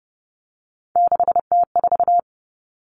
Morse code, heard as a keyed tone, about 30 wpm, 700 Hz. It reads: 6T4
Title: Common Cut Numbers - 3-Digit 30wpm